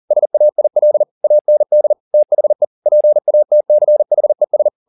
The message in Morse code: SAIL AND THE PATCHES

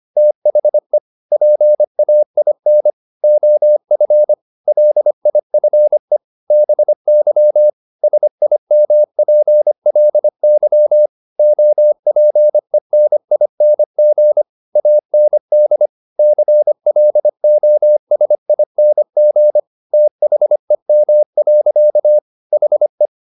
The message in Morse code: THE PAIN OF LIFE BY SIMPLY OPENING AND CLOSING THEM. HE

